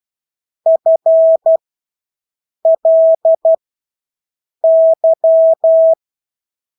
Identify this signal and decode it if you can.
Morse code, heard as a keyed tone, about 12 wpm, spaced as for 6 wpm, 650 Hz. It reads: FLY
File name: Engelske ord 12-6 wpm